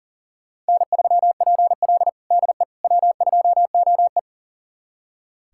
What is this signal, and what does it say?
Morse code, heard as a keyed tone, about 40 wpm, 700 Hz. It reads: N3PL DE W2OE
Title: USA Callsign DE Callsign - 4 Characters - 40wpm